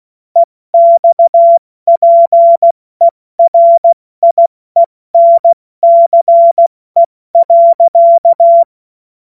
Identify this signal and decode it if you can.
Morse code, heard as a keyed tone, about 16 wpm, spaced as for 14 wpm, 680 Hz. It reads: EXPERIENCE.
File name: Morsekurs / Witze / Witz 23 Januar 2024 70 F80